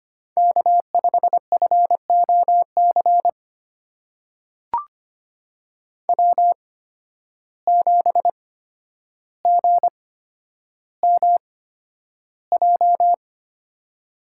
Morse code, heard as a keyed tone, about 25 wpm, spaced as for 11 wpm, 700 Hz. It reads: K5FOC W7GMJ
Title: USA Call Signs - 5 Characters - 8x Character Spacing - 25wpm